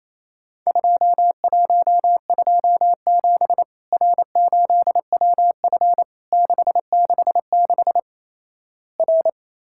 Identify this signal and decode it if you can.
Morse code, heard as a keyed tone, about 28 wpm, 700 Hz. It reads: 2127 R8WF 666 R
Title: Worked All Europe DX Contest - QTC Exchange - 28wpm